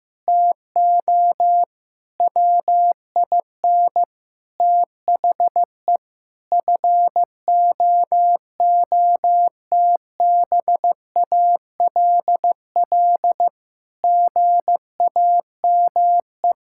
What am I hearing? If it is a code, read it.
Morse code, heard as a keyed tone, about 15 wpm, 700 Hz. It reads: TO WIN THE FOOTBALL GAME
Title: Sentences from Top 1600 Words 15wpm